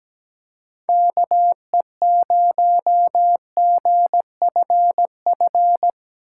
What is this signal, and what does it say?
Morse code, heard as a keyed tone, about 17 wpm, 700 Hz. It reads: KE0GFF
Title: USA Call Signs - 6 Characters - 6x Character Spacing - 17wpm